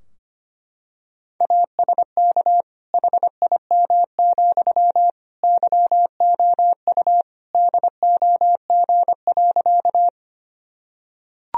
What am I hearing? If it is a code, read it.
Morse code, heard as a keyed tone, about 25 wpm, 700 Hz. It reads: ASK HIM, YOU DOG.
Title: Sentences from Top 1000 Words - Review Part 2 - 25wpm